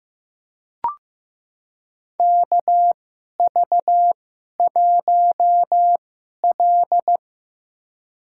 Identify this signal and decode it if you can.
Morse code, heard as a keyed tone, about 15 wpm, spaced as for 11 wpm, 700 Hz. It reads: KV1L
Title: USA Call Signs - 2x Character Spacing - 15wpm